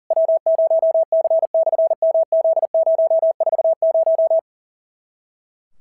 Morse code, heard as a keyed tone, about 40 wpm, 650 Hz. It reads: W0C/MZ040
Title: SOTA QSO Practice 40wpm